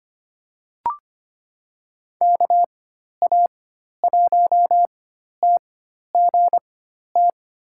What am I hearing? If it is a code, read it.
Morse code, heard as a keyed tone, about 25 wpm, spaced as for 12 wpm, 700 Hz. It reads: KA1TGT
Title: USA Call Signs - 6 Characters - 4x Character Spacing - 25wpm